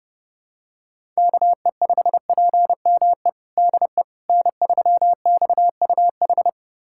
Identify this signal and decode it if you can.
Morse code, heard as a keyed tone, about 32 wpm, 700 Hz. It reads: KE5PME DE N3XUH
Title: USA Callsign DE Callsign 30wpm